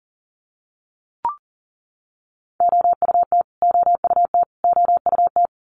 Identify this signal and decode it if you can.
Morse code, heard as a keyed tone, about 40 wpm, 700 Hz. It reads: OUT OUT OUT
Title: Top 200 Words Repeated 40wpm